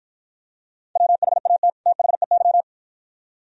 Morse code, heard as a keyed tone, about 49 wpm, 690 Hz. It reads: WHAT THEY